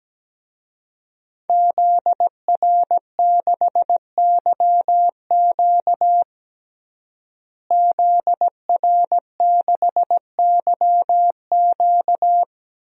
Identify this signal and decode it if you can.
Morse code, heard as a keyed tone, about 17 wpm, 700 Hz. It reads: ZR6YQ ZR6YQ
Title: DXCC-100 Call Signs - 5 Characters - Repeated 3x - 17wpm